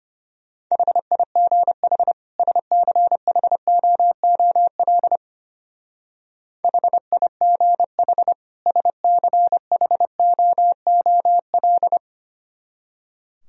Morse code, heard as a keyed tone, about 31 wpm, 700 Hz. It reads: HIGH SCHOOL HIGH SCHOOL
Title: Speed-Racing - Sets of 2 Words 25wpm